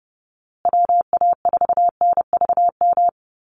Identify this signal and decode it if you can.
Morse code, heard as a keyed tone, about 30 wpm, 700 Hz. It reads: WA4NVM
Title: USA Call Signs Repeated 2x 30wpm